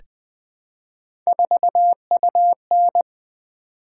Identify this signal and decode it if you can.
Morse code, heard as a keyed tone, about 20 wpm, 700 Hz. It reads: 4UN
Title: Three Letter-Number Combinations - 20wpm